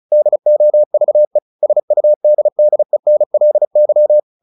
Morse code, heard as a keyed tone, about 34 wpm, 600 Hz. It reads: DOVE SUDDENLY
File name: the old man and the sea / the old man and the sea 1 35wpm